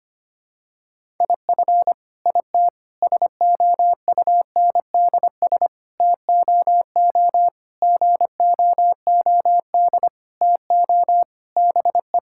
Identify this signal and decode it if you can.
Morse code, heard as a keyed tone, about 25 wpm, 700 Hz. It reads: IF IT SOUNDS TOO GOOD TO BE
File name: Sets of 9 Words - 25wpm